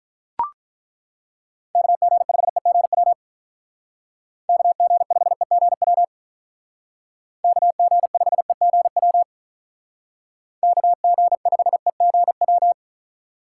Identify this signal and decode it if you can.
Morse code, heard as a keyed tone, about 43 wpm, 700 Hz. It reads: KG5EGW KG5EGW KG5EGW KG5EGW
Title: Speed Racing - USA 2x3 Call Signs 35wpm